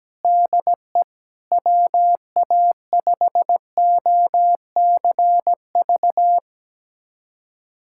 Morse code, heard as a keyed tone, about 17 wpm, 700 Hz. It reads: DE WA5OCV
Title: USA Callsign DE Callsign - 6 Characters - 17wpm